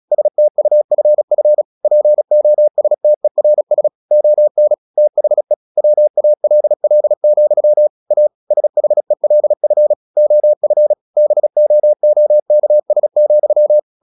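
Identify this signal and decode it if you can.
Morse code, heard as a keyed tone, about 37 wpm, 600 Hz. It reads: STUFF POSTERS ON THE WALL, A SHELF OF BOOKS,
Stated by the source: Morse practice at 36 wpm, text of Ken Follett's novel Never, from CW med Gnister.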